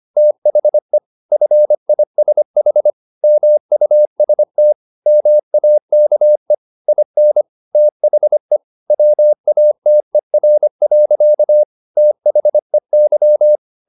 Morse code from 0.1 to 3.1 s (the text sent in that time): THE FISH